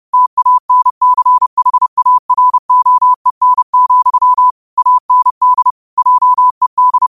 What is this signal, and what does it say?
Morse code, heard as a keyed tone, about 30 wpm, 1000 Hz. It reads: TANCHAROEN, AND JED